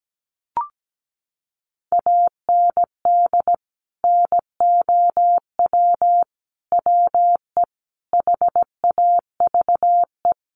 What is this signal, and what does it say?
Morse code, heard as a keyed tone, about 17 wpm, 700 Hz. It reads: AND NOW WE HAVE
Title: Sentences from Top 1500 Words 17wpm